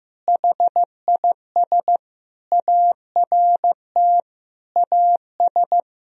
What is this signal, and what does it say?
Morse code, heard as a keyed tone, about 15 wpm, 700 Hz. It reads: HIS ART AS
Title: Sentences from Top 1000 Words - Review Part 3 - 15wpm